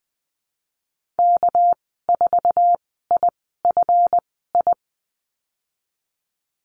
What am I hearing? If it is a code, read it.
Morse code, heard as a keyed tone, about 20 wpm, spaced as for 15 wpm, 700 Hz. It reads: K4IFI
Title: 245 Lesson 39g - Letter z - Call Signs 20wpm